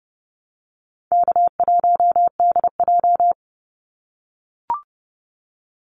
Morse code, heard as a keyed tone, about 30 wpm, 700 Hz. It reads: K1DJ
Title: Easy Call Signs - 30wpm